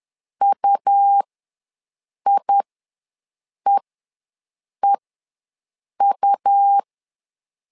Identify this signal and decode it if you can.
Morse code, heard as a keyed tone, about 10 wpm, spaced as for 6 wpm, 800 Hz. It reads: UIEEU
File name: LEZIONI / 2 UHDN / 2 UH ws 10wpm 2029